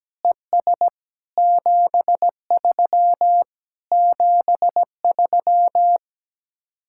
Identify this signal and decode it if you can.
Morse code, heard as a keyed tone, about 17 wpm, 700 Hz. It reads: ES 73 73
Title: Speed-Racing Common QSO Phrases - 17wpm